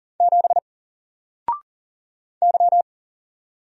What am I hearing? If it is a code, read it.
Morse code, heard as a keyed tone, about 39 wpm, 700 Hz. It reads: 7 Y
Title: Single Character (Letter:Number:Punctuation:Prosign) 40wpm